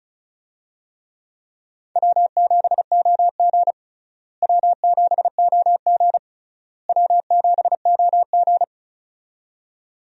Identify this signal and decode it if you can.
Morse code, heard as a keyed tone, about 36 wpm, 700 Hz. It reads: W7OG W7OG W7OG
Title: USA Call Signs - 4 Characters - Repeated 3x - 35wpm